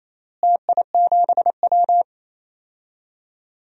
Morse code, heard as a keyed tone, about 27 wpm, 700 Hz. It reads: TI7W
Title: DXCC-100 Call Signs - 4 Characters - Repeated 3x - 28wpm